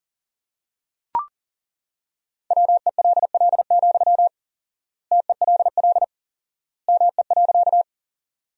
Morse code, heard as a keyed tone, about 40 wpm, 700 Hz. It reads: WELL, TELL ME.
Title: 4x Word-Spacing - Sentences from Top 200 Words - 40wpm